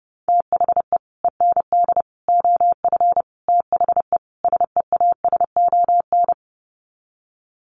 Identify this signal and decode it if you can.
Morse code, heard as a keyed tone, about 30 wpm, 700 Hz. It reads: THE END OF THE SEASON
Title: Sets of 5 Words Encore 30wpm